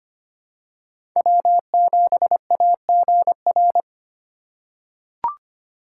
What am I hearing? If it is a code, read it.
Morse code, heard as a keyed tone, about 25 wpm, 700 Hz. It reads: W7AGR
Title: USA Call Signs - 5 Characters - 2x Character Spacing - 25wpm